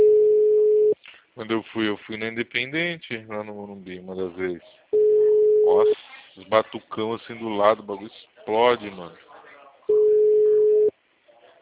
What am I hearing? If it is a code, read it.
Ringback tone (Europe, 425 Hz), with speech.